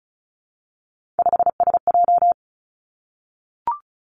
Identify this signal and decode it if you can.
Morse code, heard as a keyed tone, about 36 wpm, 700 Hz. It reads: 5SJ
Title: Three Challenging Letter-Number Combinations 35wpm